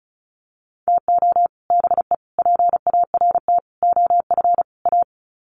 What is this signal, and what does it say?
Morse code, heard as a keyed tone, about 34 wpm, 700 Hz. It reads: TO BE PART OF A
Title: Sets of 5 Words Encore2 35wpm